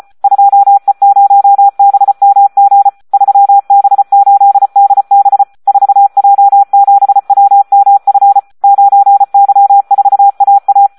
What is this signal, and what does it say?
Morse code, heard as a keyed tone, about 34 wpm, 800 Hz, with radio noise. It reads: 2E0BMG 3B8DB 4J7WMF 9Y4AA
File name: Audio practica CW / SD 1 34